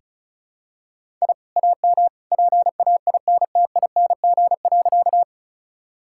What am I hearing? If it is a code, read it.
Morse code, heard as a keyed tone, about 36 wpm, 700 Hz. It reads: I AM PAINTING.
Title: Sentences from Top 1000 Words - Review Part 6 - 35wpm